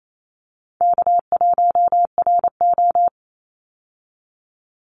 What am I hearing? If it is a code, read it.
Morse code, heard as a keyed tone, about 28 wpm, 700 Hz. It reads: K1RO
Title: Easy Call Signs Repeated 2x - 28wpm